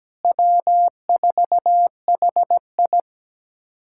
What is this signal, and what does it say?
Morse code, heard as a keyed tone, about 17 wpm, 700 Hz. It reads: W4HI
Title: USA Callsign DE Callsign 17wpm